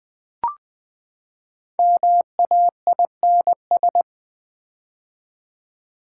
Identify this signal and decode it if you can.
Morse code, heard as a keyed tone, about 20 wpm, 700 Hz. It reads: MAINS
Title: Build A Word - 5 Letters At A Time - 10-Letter Words - 20wpm